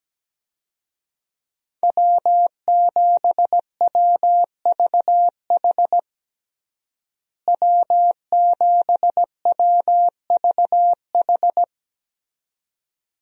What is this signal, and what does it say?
Morse code, heard as a keyed tone, about 17 wpm, 700 Hz. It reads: W7WVH W7WVH
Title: USA Call Signs - 5 Characters - Repeated 2x - 17wpm